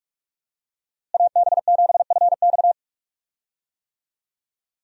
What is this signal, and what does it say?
Morse code, heard as a keyed tone, about 46 wpm, 700 Hz. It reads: AB7FX
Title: USA Call Signs - 5 Characters - Repeated 3x - 45wpm